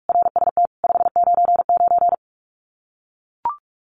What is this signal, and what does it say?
Morse code, heard as a keyed tone, about 44 wpm, 700 Hz. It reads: RST 599
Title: Common QSO Phrases 45wpm